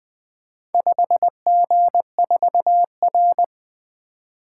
Morse code, heard as a keyed tone, about 20 wpm, 700 Hz. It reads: 5G4R
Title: Four Letter-Number Combinations - 20wpm